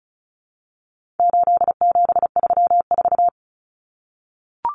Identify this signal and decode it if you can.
Morse code, heard as a keyed tone, about 34 wpm, 700 Hz. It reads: 8734
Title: Common Cut Numbers - 4-Digit 35wpm